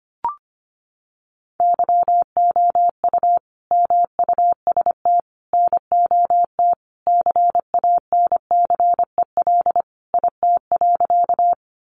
Morse code, heard as a keyed tone, about 25 wpm, 700 Hz. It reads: YOU MUST NOT CANCEL IT.
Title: Sentences from Top 1200 Words 25wpm